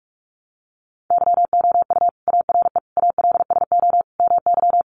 Morse code, heard as a keyed tone, about 46 wpm, 700 Hz. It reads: YOU ARE ALSO MY
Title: Sentences from Top 1300 Words 45wpm